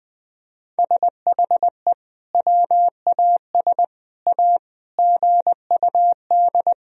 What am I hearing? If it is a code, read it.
Morse code, heard as a keyed tone, about 20 wpm, 700 Hz. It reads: SHE WAS A GUD
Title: Sentences from Top 400 Words and Abbreviations 20wpm